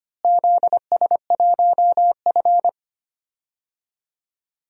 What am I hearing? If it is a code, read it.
Morse code, heard as a keyed tone, about 25 wpm, 700 Hz. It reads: ZS1F
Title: DXCC-100 Call Signs - 4 Characters - Repeated 3x - 25wpm